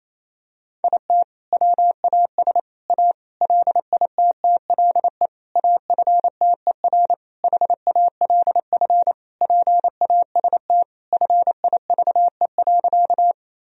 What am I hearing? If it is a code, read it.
Morse code, heard as a keyed tone, about 28 wpm, 700 Hz. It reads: IT WAS A LITTLE AFTER HALF PAST FIVE.
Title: Sentences from Top 1000 Words - Review Part 3 - 28wpm